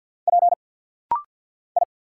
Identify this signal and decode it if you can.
Morse code, heard as a keyed tone, about 50 wpm, 690 Hz. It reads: P I
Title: Single Letters - Rapid-Fire 50wpm